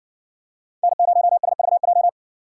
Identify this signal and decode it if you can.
Morse code, heard as a keyed tone, about 60 wpm, 700 Hz. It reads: N0SVJ